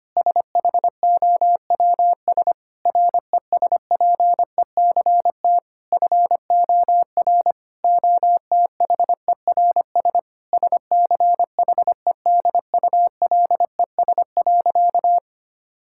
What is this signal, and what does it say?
Morse code, heard as a keyed tone, about 25 wpm, 700 Hz. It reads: SHOWS RESPECT FOR OTHERS SCHEDULES.
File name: practice / 25wpm / Difficult Sentences with Hints - 25wpm